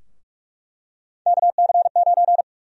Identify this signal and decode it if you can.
Morse code, heard as a keyed tone, about 46 wpm, 700 Hz. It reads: KC9
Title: USA Call Sign Prefixes - Repeated 2x - 45wpm